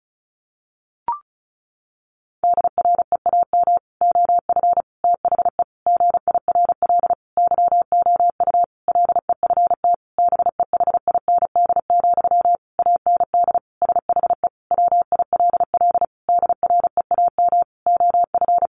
Morse code, heard as a keyed tone, about 34 wpm, 700 Hz. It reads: DREAM OF THE GIRL YOU LEFT BEHIND, AND SHE WILL DREAM OF
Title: Sentences from Top 1500 Words 35wpm